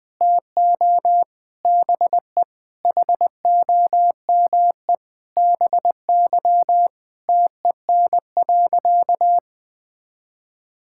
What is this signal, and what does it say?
Morse code, heard as a keyed tone, about 20 wpm, 700 Hz. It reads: TO BE HOME BY TEN.
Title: Sentences with 4-Word Phrases - 20wpm